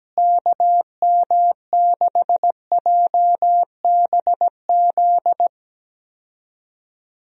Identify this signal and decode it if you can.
Morse code, heard as a keyed tone, about 17 wpm, 700 Hz. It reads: KM6JBZ
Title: USA Call Signs - 6 Characters - Repeated 2x - 17wpm